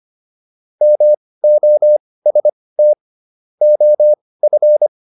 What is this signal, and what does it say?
Morse code, heard as a keyed tone, about 25 wpm, spaced as for 18 wpm, 600 Hz. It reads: MOST OF